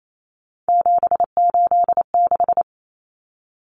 Morse code, heard as a keyed tone, about 28 wpm, 700 Hz. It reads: 786
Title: Common Cut Numbers - 3-Digit 28wpm